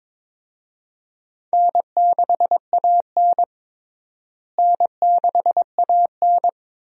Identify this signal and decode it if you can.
Morse code, heard as a keyed tone, about 22 wpm, 700 Hz. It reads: N6AN N6AN
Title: USA Call Signs - 4 Characters - Repeated 2x - 22wpm